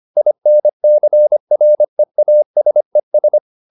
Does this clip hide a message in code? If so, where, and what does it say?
Yes: Morse code from 0.2 to 3.4 s, heard as a keyed tone, about 25 wpm, 600 Hz. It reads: INCREASES